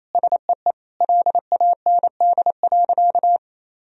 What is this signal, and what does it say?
Morse code, heard as a keyed tone, about 27 wpm, 700 Hz. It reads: SEE LAND.
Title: Sentences from Top 1000 Words - Review Part 2 - 28wpm